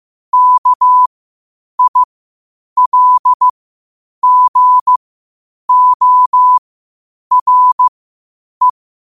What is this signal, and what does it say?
Morse code, heard as a keyed tone, about 15 wpm, spaced as for 8 wpm, 1000 Hz. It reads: KILGORE
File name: qotdfiles / 8wpm